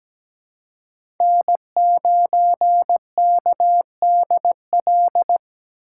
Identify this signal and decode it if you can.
Morse code, heard as a keyed tone, about 17 wpm, 700 Hz. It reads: N9KDL
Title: USA Call Signs - 5 Characters - 2x Character Spacing - 17wpm